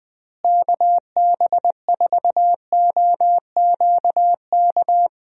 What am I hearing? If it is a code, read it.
Morse code, heard as a keyed tone, about 21 wpm, 700 Hz. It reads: KB4OQK
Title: USA Call Signs - 6 Characters - Repeated 3x - 20wpm